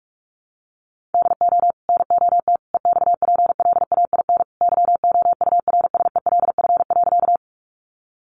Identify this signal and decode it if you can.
Morse code, heard as a keyed tone, about 44 wpm, 700 Hz. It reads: DO NOT EXPLAIN YOURSELF.
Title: Sentences from Top 1100 Words 45wpm